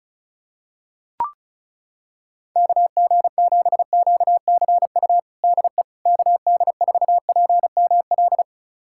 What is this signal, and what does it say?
Morse code, heard as a keyed tone, about 36 wpm, 700 Hz. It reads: KG7QCU DE KD4PML
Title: USA Callsign DE Callsign - 6 Characters - 35wpm